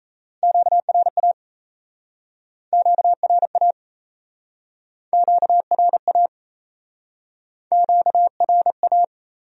Morse code, heard as a keyed tone, about 37 wpm, 700 Hz. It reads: QRA QRA QRA QRA